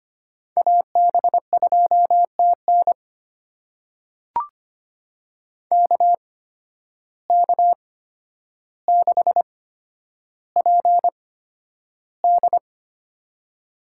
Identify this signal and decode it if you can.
Morse code, heard as a keyed tone, about 25 wpm, spaced as for 11 wpm, 700 Hz. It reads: AB2TN KK6PD
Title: USA Call Signs - 5 Characters - 8x Character Spacing - 25wpm